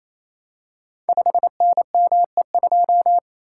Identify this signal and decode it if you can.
Morse code, heard as a keyed tone, about 28 wpm, 700 Hz. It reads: 5NME2
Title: Code Groups - 5 Letter:Numbers - 28wpm